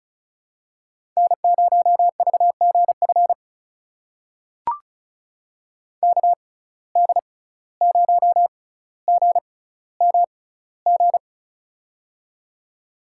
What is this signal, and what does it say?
Morse code, heard as a keyed tone, about 34 wpm, spaced as for 17 wpm, 700 Hz. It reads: N0VGF KD0GMG
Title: USA Call Signs - 6x Character Spacing - 35wpm